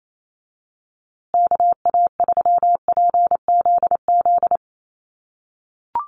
Morse code, heard as a keyed tone, about 27 wpm, 700 Hz. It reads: KA3PZZ
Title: USA Call Signs Repeated 2x 28wpm